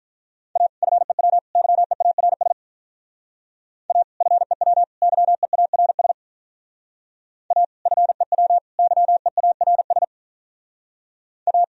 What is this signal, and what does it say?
Morse code, heard as a keyed tone, about 42 wpm, 700 Hz. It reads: A FEW YEARS A FEW YEARS A FEW YEARS A
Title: Speed-Racing - Sets of 3 Words 35wpm